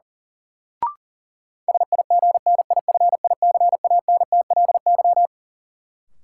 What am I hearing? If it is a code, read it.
Morse code, heard as a keyed tone, about 37 wpm, 700 Hz. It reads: SIGNIFICANTLY
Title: Sentences from Top 1500 Words - New Word Drill Alt - 40wpm